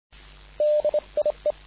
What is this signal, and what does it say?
Morse code, heard as a keyed tone, about 29 wpm, spaced as for 25 wpm, 600 Hz. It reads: DIE
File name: dk5ke / BUG-Beispiel